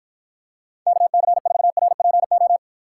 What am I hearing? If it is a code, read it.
Morse code, heard as a keyed tone, about 53 wpm, 700 Hz. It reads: KC4RPO